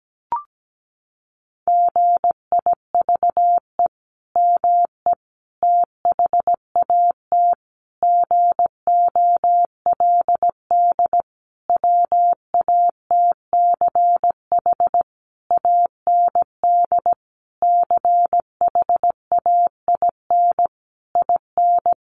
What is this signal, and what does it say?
Morse code, heard as a keyed tone, about 17 wpm, 700 Hz. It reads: GIVE ME THAT GOLD WATCH AND CHAIN IN
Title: Sentences from Top 1500 Words 17wpm